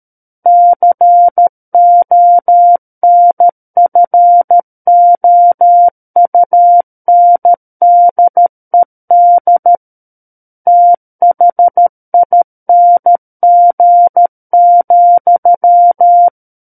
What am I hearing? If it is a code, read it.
Morse code, heard as a keyed tone, about 13 wpm, 700 Hz. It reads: CONFOUNDED THING,
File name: Book One / 04-Chapter 4 - 13 WPM